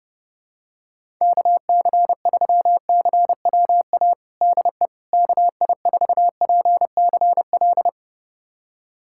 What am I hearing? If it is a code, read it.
Morse code, heard as a keyed tone, about 30 wpm, 700 Hz. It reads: KC3CWA DE KI4PCL
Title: USA Callsign DE Callsign 30wpm